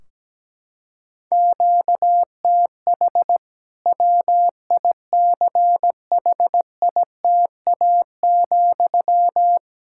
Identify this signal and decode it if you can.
Morse code, heard as a keyed tone, about 17 wpm, 700 Hz. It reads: QTH WICHITA,
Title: Practice QSO - Exchange 1 - Beginning - 17wpm